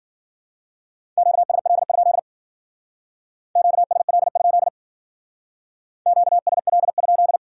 Morse code, heard as a keyed tone, about 51 wpm, 700 Hz. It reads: QSL? QSL? QSL?